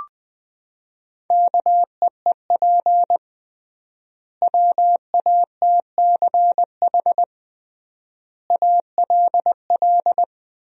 Morse code, heard as a keyed tone, about 21 wpm, 700 Hz. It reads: KEEP WATCH ALL
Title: 3x Word-Spacing - Sentences from Top 300 Words - 20wpm